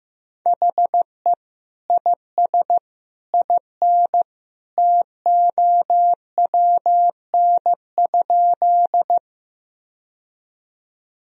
Morse code, heard as a keyed tone, about 15 wpm, 700 Hz. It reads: HE IS IN TOWN?
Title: Sentences from Top 1000 Words - Review Part 2 - 15wpm